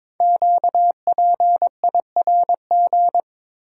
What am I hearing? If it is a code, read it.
Morse code, heard as a keyed tone, about 22 wpm, 700 Hz. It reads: QPIRG
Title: Code Groups - 5 Letters - 22wpm